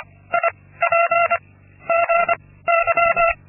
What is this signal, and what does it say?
Morse code, heard as a keyed tone, about 25 wpm, 680 Hz. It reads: I P G Y
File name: Linksmakalnis-2